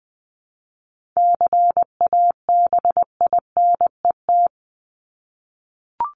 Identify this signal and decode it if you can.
Morse code, heard as a keyed tone, about 19 wpm, 700 Hz. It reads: CABINET
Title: Top 1751-2000 Words 20wpm